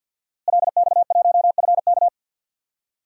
Morse code, heard as a keyed tone, about 46 wpm, 700 Hz. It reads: LX1FK
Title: DXCC-100 Call Signs - 5 Characters - Repeated 2x - 50wpm